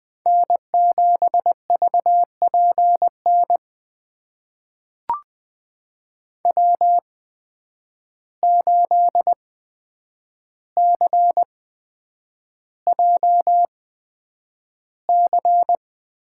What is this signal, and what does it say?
Morse code, heard as a keyed tone, about 20 wpm, spaced as for 9 wpm, 700 Hz. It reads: N7VPN W8CJC
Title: USA Call Signs - 5 Characters - 8x Character Spacing - 20wpm